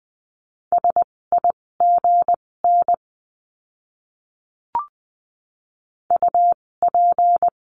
Morse code, heard as a keyed tone, about 20 wpm, spaced as for 16 wpm, 700 Hz. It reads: SIGN UP
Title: 256 Lesson 42a - Words 12-20wpm